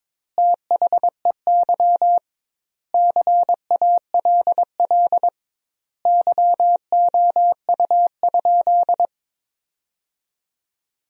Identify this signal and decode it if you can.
Morse code, heard as a keyed tone, about 21 wpm, 700 Hz. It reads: THEY CALL YOU?
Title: 2x Word-Spacing - Sentences from Top 100 Words - 22wpm